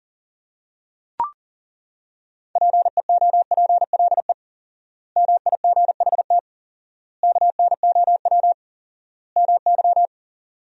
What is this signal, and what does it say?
Morse code, heard as a keyed tone, about 40 wpm, 700 Hz. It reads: PEOPLE MIGHT KNOW MY
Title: 4x Word-Spacing - Sentences from Top 500 Words - 40wpm